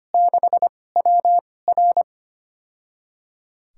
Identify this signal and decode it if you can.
Morse code, heard as a keyed tone, about 25 wpm, spaced as for 18 wpm, 700 Hz. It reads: 6WR
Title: USA Call Signs - 5 Characters - 2x Character Spacing - 25wpm